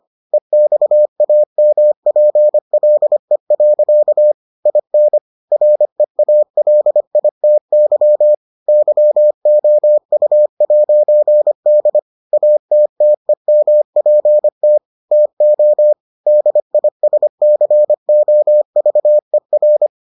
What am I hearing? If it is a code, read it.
Morse code, heard as a keyed tone, about 25 wpm, 600 Hz. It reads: EXAMPLE. IN REALITY YOU'D ATTEMPT TO DISCOVER